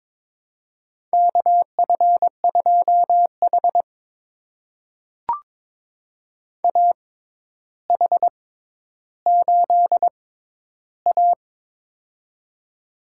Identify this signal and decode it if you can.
Morse code, heard as a keyed tone, about 22 wpm, spaced as for 11 wpm, 700 Hz. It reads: KF2H AH8A
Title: USA Call Signs - 4 Characters - 6x Character Spacing - 22wpm